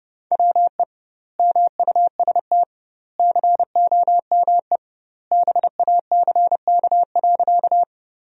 Morse code, heard as a keyed tone, about 30 wpm, 700 Hz. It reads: WE MUST COME BACK.
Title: 2x Word-Spacing - Sentences from Top 200 Words - 30wpm